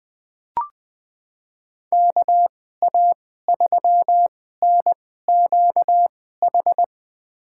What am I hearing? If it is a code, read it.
Morse code, heard as a keyed tone, about 21 wpm, spaced as for 15 wpm, 700 Hz. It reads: KA3NQH
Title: USA Call Signs - 2x Character Spacing - 20wpm